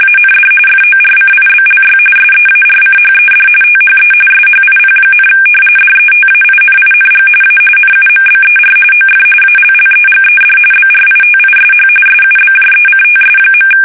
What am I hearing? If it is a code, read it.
RTTY, about 45.45 baud. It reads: AND FORCIBLY SEARCHED THE SHIP, DETAINING A NUMBER OF ALLEGED BRITISH SUBJECTS.
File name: fsk850 / usnavy1